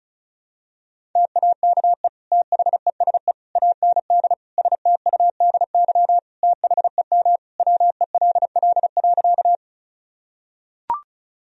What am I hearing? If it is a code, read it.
Morse code, heard as a keyed tone, about 36 wpm, 700 Hz. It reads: TAKE THESE AND STUDY THEM WELL.
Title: Sentences from Top 1000 Words - Review Part 1 - 35wpm